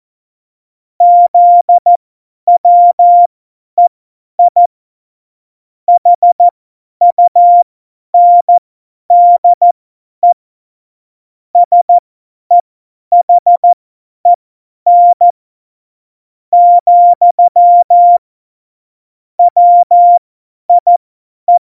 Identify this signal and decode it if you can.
Morse code, heard as a keyed tone, about 14 wpm, spaced as for 10 wpm, 700 Hz. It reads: ZWEI HUNDE SEHEN, WIE
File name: Morsekurs / Witze / Witz 05. Juni 2018, 50 F70